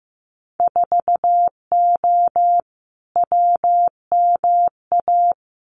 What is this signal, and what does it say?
Morse code, heard as a keyed tone, about 15 wpm, 700 Hz. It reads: 4O WMA
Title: Winter Field Day Exchanges Encore 15wpm